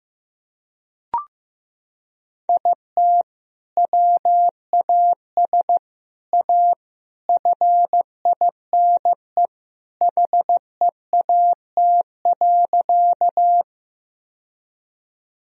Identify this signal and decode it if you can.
Morse code, heard as a keyed tone, about 16 wpm, 700 Hz. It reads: IT WAS A FINE HEAT.
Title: Sentences from Top 1000 Words - Review Part 3 - 15wpm